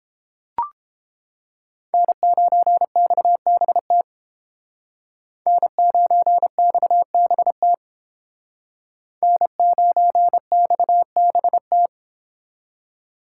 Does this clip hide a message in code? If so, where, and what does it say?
Yes: Morse code from 1.9 to 11.9 s, heard as a keyed tone, about 31 wpm, 700 Hz. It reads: N9XBT N9XBT N9XBT